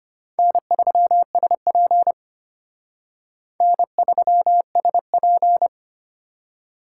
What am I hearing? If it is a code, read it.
Morse code, heard as a keyed tone, about 29 wpm, 700 Hz. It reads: N3SP N3SP
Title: Speed Racing - USA Call Signs 25wpm